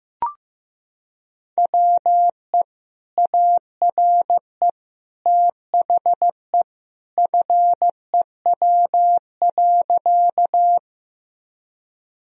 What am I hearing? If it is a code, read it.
Morse code, heard as a keyed tone, about 16 wpm, 700 Hz. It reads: WE ARE THE FEW.
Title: Sentences from Top 1000 Words - Review Part 1 - 15wpm